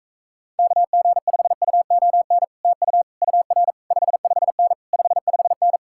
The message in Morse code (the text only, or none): KG5UON TU UR 55N 55N